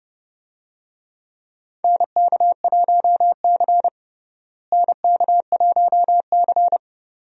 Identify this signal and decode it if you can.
Morse code, heard as a keyed tone, about 32 wpm, 700 Hz. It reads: NK1C NK1C
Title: USA Call Signs - 4 Characters - Repeated 2x - 30wpm